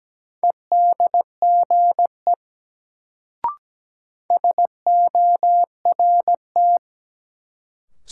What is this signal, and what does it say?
Morse code, heard as a keyed tone, about 17 wpm, 700 Hz. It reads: EDGE SORT
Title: Sentences from Top 1300 Words - New Word Drill Alt - 17wpm